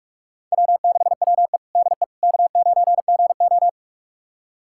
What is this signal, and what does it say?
Morse code, heard as a keyed tone, about 47 wpm, 700 Hz. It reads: W6WE DE K9GO
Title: USA Callsign DE Callsign - 4 Characters - 45wpm